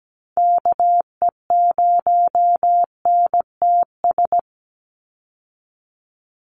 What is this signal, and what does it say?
Morse code, heard as a keyed tone, about 17 wpm, 700 Hz. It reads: KE0NTS
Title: USA Call Signs 17wpm